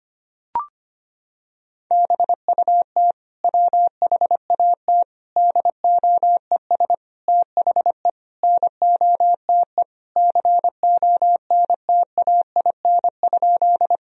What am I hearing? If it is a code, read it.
Morse code, heard as a keyed tone, about 25 wpm, 700 Hz. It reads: BUT WHAT DOES THE NOTE CONTAIN?
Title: Sentences from Top 1000 Words - Review Part 2 - 25wpm